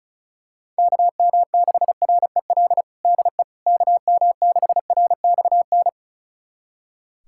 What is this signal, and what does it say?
Morse code, heard as a keyed tone, about 36 wpm, 700 Hz. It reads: KM6REL DE KM6RXN
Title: USA Callsign DE Callsign - 6 Characters - 35wpm